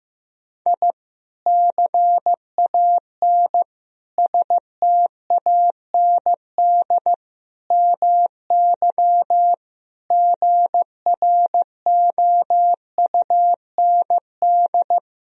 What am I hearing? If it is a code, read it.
Morse code, heard as a keyed tone, about 15 wpm, 700 Hz. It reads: I CAN STAND MY GROUND
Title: Sentences from Top 1000 Words - Review Part 2 - 15wpm